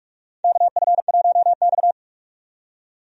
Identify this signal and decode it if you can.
Morse code, heard as a keyed tone, about 46 wpm, 700 Hz. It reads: KF1X
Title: USA Call Signs - 4 Characters - 45wpm